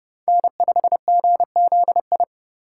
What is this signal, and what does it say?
Morse code, heard as a keyed tone, about 32 wpm, 700 Hz. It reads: N5GZI
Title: USA Call Signs - 2x Character Spacing - 30wpm